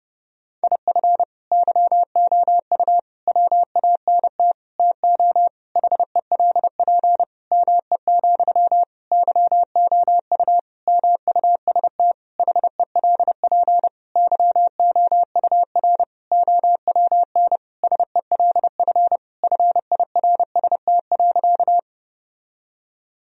Morse code, heard as a keyed tone, about 30 wpm, 700 Hz. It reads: IF YOU WANT TO HELP ME, YOU MUST HELP YOUR OWN SELF FIRST.
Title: Sentences from Top 1000 Words - Review Part 1 - 30wpm